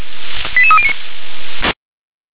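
Five-tone selective call (ZVEI), heard as quick runs of five tones, tones of about 80 ms. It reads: 88309
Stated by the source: ZVEI five-tone selective call received on an SDR.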